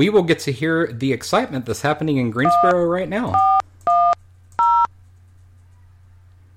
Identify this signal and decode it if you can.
DTMF, heard as telephone touch tones, tones of about 290 ms, with speech. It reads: 1410